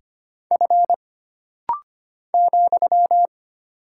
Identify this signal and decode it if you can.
Morse code, heard as a keyed tone, about 25 wpm, 700 Hz. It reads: F ,
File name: Single Character (Letter:Number:Punctuation:Prosign) - Rapid-Fire - 25wpm